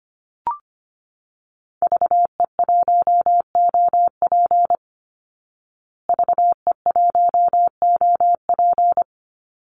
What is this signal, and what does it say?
Morse code, heard as a keyed tone, about 25 wpm, 700 Hz. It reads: VE1OP VE1OP
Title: Call Signs Repeated 3x - 25wpm